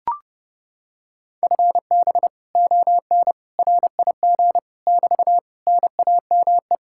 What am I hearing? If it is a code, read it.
Morse code, heard as a keyed tone, about 29 wpm, 700 Hz. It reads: FB ON RIG <BT> NAME